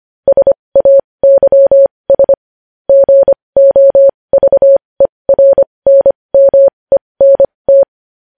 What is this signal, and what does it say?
Morse code, heard as a keyed tone, about 25 wpm, spaced as for 20 wpm, 550 Hz. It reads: SAYS GOVERNMENT
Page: news headlines in morse code at 20 wpm